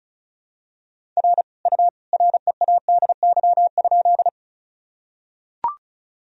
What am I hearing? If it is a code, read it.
Morse code, heard as a keyed tone, about 36 wpm, 700 Hz. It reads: R U READY?
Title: Sentences from Top 400 Words and Abbreviations 35wpm